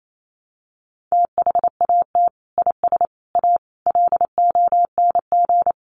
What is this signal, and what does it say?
Morse code, heard as a keyed tone, about 28 wpm, 700 Hz. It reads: THAT IS A LONG
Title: Sentences from Top 100 Words 28wpm